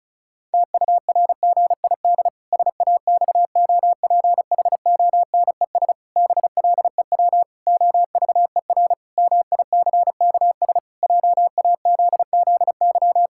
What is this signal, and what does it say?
Morse code, heard as a keyed tone, about 36 wpm, 700 Hz. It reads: TURGID SAXOPHONES BLEW OVER MICKS JAZZY
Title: Pangram Challenge v2 - 35wpm